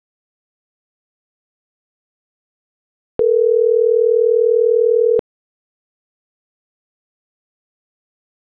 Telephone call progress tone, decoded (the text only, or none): ringback tone